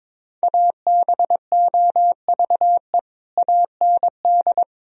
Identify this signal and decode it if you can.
Morse code, heard as a keyed tone, about 22 wpm, 700 Hz. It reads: ABOVE AND
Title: Sentences from Top 1000 Words - Review Part 2 - 22wpm